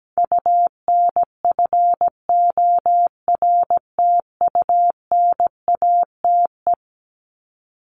Morse code, heard as a keyed tone, about 17 wpm, 700 Hz. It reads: UNFORTUNATE
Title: LY Suffix - Top 100 Words with Base Words 17wpm